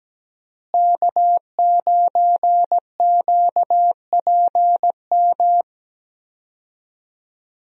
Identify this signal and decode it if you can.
Morse code, heard as a keyed tone, about 17 wpm, 700 Hz. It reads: K9QPM
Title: USA Call Signs - 5 Characters - Repeated 2x - 17wpm